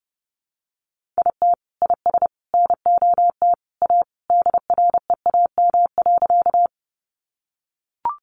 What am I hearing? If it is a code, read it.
Morse code, heard as a keyed tone, about 30 wpm, 700 Hz. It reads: IT IS NOT A DREAM.
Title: Sentences from Top 1500 Words 30wpm